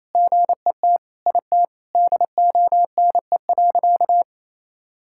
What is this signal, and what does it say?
Morse code, heard as a keyed tone, about 28 wpm, 700 Hz. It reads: GET IT DONE.
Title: Sentences from Top 1000 Words - Review Part 5 - 28wpm